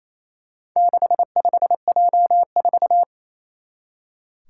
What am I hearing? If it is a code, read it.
Morse code, heard as a keyed tone, about 28 wpm, 700 Hz. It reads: 65J4
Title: Four Challenging Letter-Number Combinations 28wpm